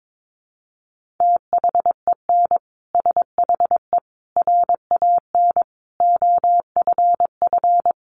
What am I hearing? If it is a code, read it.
Morse code, heard as a keyed tone, about 22 wpm, 700 Hz. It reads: THEN SHE RAN OFF
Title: Sentences from Top 1400 Words 22wpm